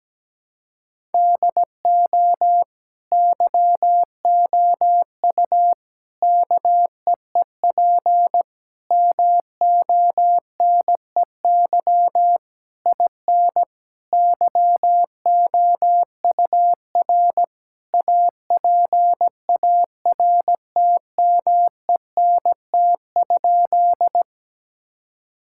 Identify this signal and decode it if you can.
Morse code, heard as a keyed tone, about 17 wpm, 700 Hz. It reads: DO YOU KEEP MONEY IN YOUR APARTMENT?
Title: Sentences from Top 1000 Words - Review Part 6 - 17wpm